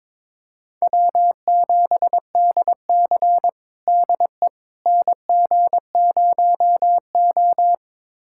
Morse code, heard as a keyed tone, about 22 wpm, 700 Hz. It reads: W7DC DE NG0O
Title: USA Callsign DE Callsign - 4 Characters - 22wpm